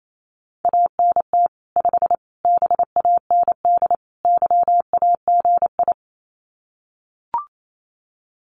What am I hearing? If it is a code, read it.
Morse code, heard as a keyed tone, about 28 wpm, 700 Hz. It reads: ANT 5 BAND YAGI
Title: Common QSO Phrases 28wpm